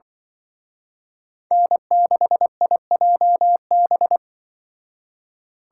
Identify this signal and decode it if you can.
Morse code, heard as a keyed tone, about 24 wpm, 700 Hz. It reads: N6IJB
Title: Speed Racing - USA 1x3 Call Signs 20wpm